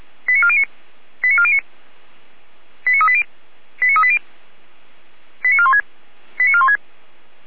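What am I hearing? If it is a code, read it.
Five-tone selective call (ZVEI), heard as quick runs of five tones, tones of about 70 ms. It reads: 89409 89409 89380 89380 89416 89416
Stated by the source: ZVEI five-tone selective call received on SDR.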